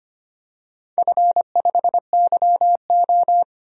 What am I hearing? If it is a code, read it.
Morse code, heard as a keyed tone, about 25 wpm, 700 Hz. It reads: F5YO
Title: DXCC-100 Call Signs - 4 Characters - 25wpm